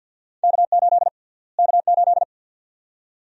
Morse code, heard as a keyed tone, about 52 wpm, 700 Hz. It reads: K8 K8
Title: USA Call Sign Prefixes - Repeated 2x - 50wpm